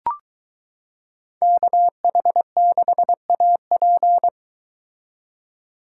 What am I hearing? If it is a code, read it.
Morse code, heard as a keyed tone, about 23 wpm, 700 Hz. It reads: KH6AP